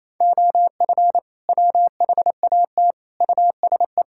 Morse code, heard as a keyed tone, about 28 wpm, 700 Hz. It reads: OF WHAT USE
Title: Sentences from Top 1000 Words - Review Part 5 - 28wpm